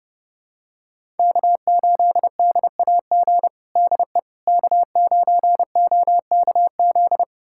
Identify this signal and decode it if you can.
Morse code, heard as a keyed tone, about 30 wpm, 700 Hz. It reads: K8DAG DE K9OKZ
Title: USA Callsign DE Callsign - 5 Characters - 30wpm